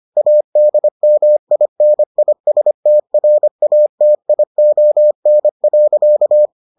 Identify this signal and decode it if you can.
Morse code, heard as a keyed tone, about 25 wpm, 600 Hz. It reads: ADMINISTRATION.